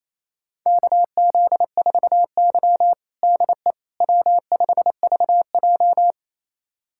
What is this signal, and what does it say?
Morse code, heard as a keyed tone, about 28 wpm, 700 Hz. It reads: KZ4Y DE W5VJ
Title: USA Callsign DE Callsign - 4 Characters - 28wpm